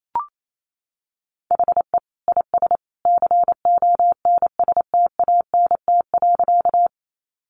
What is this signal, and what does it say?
Morse code, heard as a keyed tone, about 28 wpm, 700 Hz. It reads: HE IS CONSTANT.
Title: Sentences from Top 1500 Words 28wpm